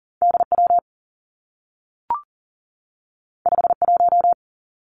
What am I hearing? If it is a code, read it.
Morse code, heard as a keyed tone, about 40 wpm, 700 Hz. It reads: DW 51
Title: Two Challenging Letter-Number Combinations 40wpm